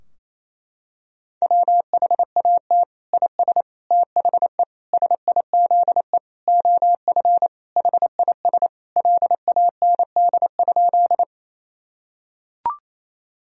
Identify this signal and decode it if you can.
Morse code, heard as a keyed tone, about 28 wpm, 700 Hz. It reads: WHAT IS THE SIZE OF HIS LAND?
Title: Sentences from Top 1000 Words - Review Part 2 - 28wpm